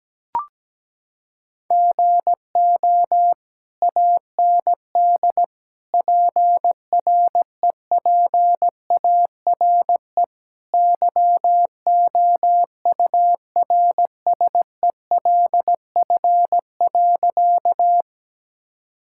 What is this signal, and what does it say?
Morse code, heard as a keyed tone, about 17 wpm, 700 Hz. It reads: GO AND PREPARE YOURSELF.
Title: Sentences from Top 1000 Words - Review Part 6 - 17wpm